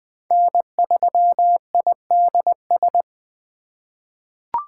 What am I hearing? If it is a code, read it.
Morse code, heard as a keyed tone, about 21 wpm, 700 Hz. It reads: N3IDS
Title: USA Call Signs - 4x Character Spacing - 20wpm